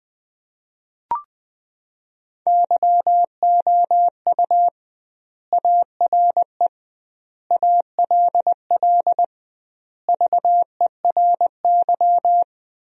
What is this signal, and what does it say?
Morse code, heard as a keyed tone, about 20 wpm, 700 Hz. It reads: YOU ARE ALL VERY
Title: 2x Word-Spacing - Sentences from Top 200 Words - 20wpm